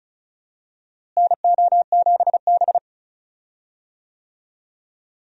Morse code, heard as a keyed tone, about 36 wpm, 700 Hz. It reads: NO7B
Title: USA Call Signs - 4 Characters - Repeated 2x - 35wpm